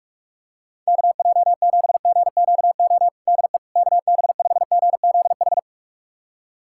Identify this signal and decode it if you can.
Morse code, heard as a keyed tone, about 46 wpm, 700 Hz. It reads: KJ7GQO DE KB5GZH
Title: USA Callsign DE Callsign - 6 Characters - 45wpm